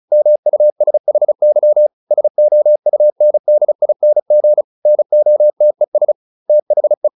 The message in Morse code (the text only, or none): MUSHY SOUNDING NOTES THE